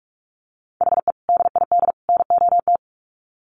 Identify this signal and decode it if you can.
Morse code, heard as a keyed tone, about 44 wpm, 700 Hz. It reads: HE DID NOT
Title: Sets of 3 Words 45wpm